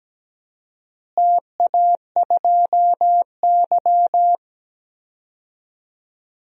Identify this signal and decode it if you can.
Morse code, heard as a keyed tone, about 17 wpm, 700 Hz. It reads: TA2Y
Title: DXCC-100 Call Signs - 4 Characters - Repeated 2x - 17wpm